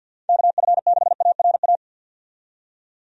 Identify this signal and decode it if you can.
Morse code, heard as a keyed tone, about 51 wpm, 700 Hz. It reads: KF6ARA
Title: USA Call Signs - 6 Characters - 50wpm